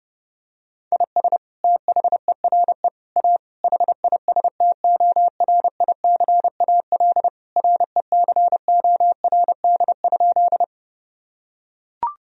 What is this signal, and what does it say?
Morse code, heard as a keyed tone, about 28 wpm, 700 Hz. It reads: IS THERE A HISTORICAL RECORD?
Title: Sentences from Top 1000 Words - Review Part 4 - 30wpm